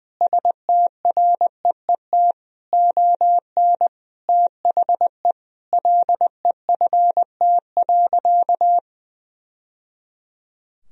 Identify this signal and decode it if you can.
Morse code, heard as a keyed tone, about 20 wpm, 700 Hz. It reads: STREET ON THE LEFT.
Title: Sentences from Top 1000 Words - Review Part 3 - 20wpm